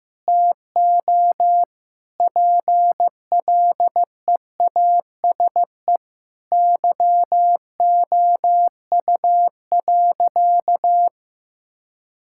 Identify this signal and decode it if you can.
Morse code, heard as a keyed tone, about 15 wpm, 700 Hz. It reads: TO PLEASE YOU.
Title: Sentences from Top 1600 Words 15wpm